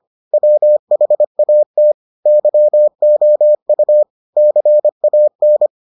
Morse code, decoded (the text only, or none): WHAT YOU CAN